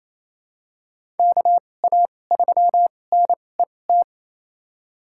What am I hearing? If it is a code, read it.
Morse code, heard as a keyed tone, about 27 wpm, spaced as for 20 wpm, 700 Hz. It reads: KA3NET
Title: USA Call Signs - 6 Characters - 2x Character Spacing - 28wpm